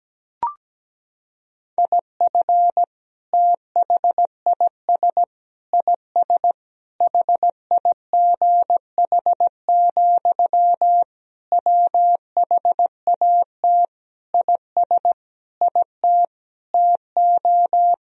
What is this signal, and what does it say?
Morse code, heard as a keyed tone, about 17 wpm, 700 Hz. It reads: IF THIS IS HIGH, WHAT IS IT TO
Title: Sentences from Top 1000 Words - Review Part 1 - 17wpm